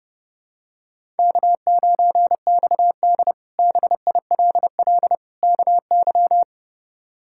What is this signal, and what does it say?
Morse code, heard as a keyed tone, about 30 wpm, 700 Hz. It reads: K9XD BILL KY
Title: K1USN SST QSO Practice - 30wpm